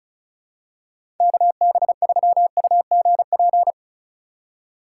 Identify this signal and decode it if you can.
Morse code, heard as a keyed tone, about 34 wpm, 700 Hz. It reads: KB3UGP
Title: USA Call Signs - 4x Character Spacing - 35wpm